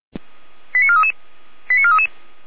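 Five-tone selective call (ZVEI), heard as quick runs of five tones, tones of about 70 ms. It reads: 89433 89433
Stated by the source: ZVEI five-tone fire brigade alert received by SDR.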